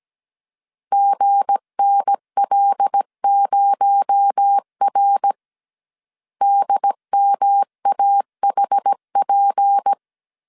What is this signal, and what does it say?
Morse code, heard as a keyed tone, about 17 wpm, 800 Hz. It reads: GNL0R DMAHP
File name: CW / DET / 1 9 1500C S C 16wpm 2909